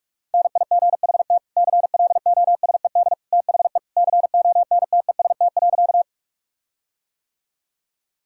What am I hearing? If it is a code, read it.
Morse code, heard as a keyed tone, about 45 wpm, 700 Hz. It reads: NIGHT CLOSED THE CONTEST.
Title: Sentences from Top 1600 Words 45wpm